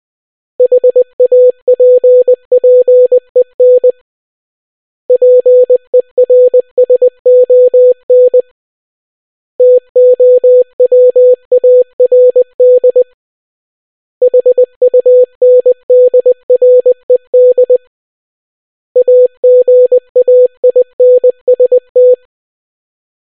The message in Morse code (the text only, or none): HAPPEN PERSON TOWARD HUNDRED AGAINST